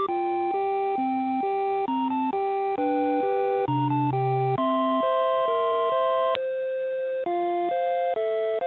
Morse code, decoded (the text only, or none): none